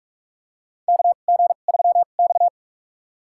Morse code, heard as a keyed tone, about 46 wpm, spaced as for 33 wpm, 700 Hz. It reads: KG3X
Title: USA Call Signs - 4 Characters - 2x Character Spacing - 45wpm